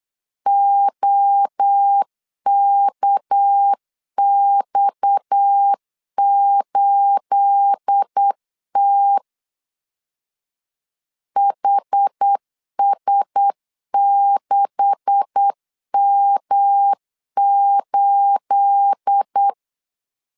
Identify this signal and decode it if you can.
Morse code, heard as a keyed tone, about 8 wpm, 800 Hz. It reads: OKX8T HS6M8
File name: CW / DET / 1 9 1500C S C 8wpm 5653